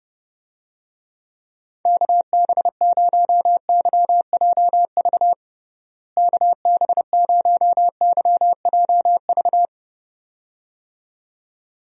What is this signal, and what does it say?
Morse code, heard as a keyed tone, about 32 wpm, 700 Hz. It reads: KB0YJV KB0YJV
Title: USA Call Signs - 6 Characters - Repeated 2x - 30wpm